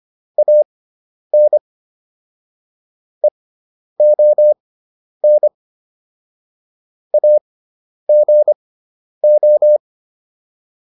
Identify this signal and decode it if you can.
Morse code, heard as a keyed tone, about 25 wpm, spaced as for 10 wpm, 600 Hz. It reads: AN EON AGO